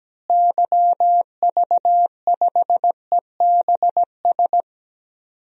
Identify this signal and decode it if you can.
Morse code, heard as a keyed tone, about 17 wpm, 700 Hz. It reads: YV5EBS
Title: DXCC-100 Call Signs - 6 Characters - Repeated 2x - 17wpm